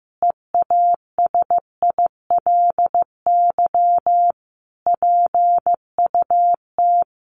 Morse code, heard as a keyed tone, about 15 wpm, 700 Hz. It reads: EASILY PUT
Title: Sentences from Top 1400 Words 15wpm